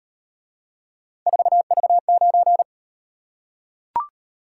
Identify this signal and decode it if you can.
Morse code, heard as a keyed tone, about 38 wpm, 700 Hz. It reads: 4V9